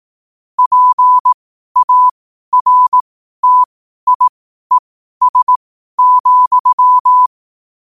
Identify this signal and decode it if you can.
Morse code, heard as a keyed tone, about 18 wpm, spaced as for 12 wpm, 1000 Hz. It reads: PARTIES,
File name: qotdfiles / 12wpm